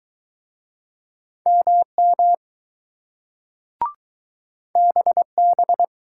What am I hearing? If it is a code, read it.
Morse code, heard as a keyed tone, about 20 wpm, 700 Hz. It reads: MM BB